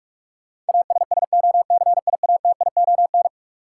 Morse code, heard as a keyed tone, about 46 wpm, 700 Hz. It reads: ASSOCIATION
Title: Sentences from Top 1100 Words - New Word Drill - 45wpm